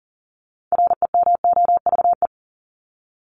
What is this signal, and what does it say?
Morse code, heard as a keyed tone, about 40 wpm, 700 Hz. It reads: REMOVE
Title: ED Suffix - Top 100 Words with Base Words 40wpm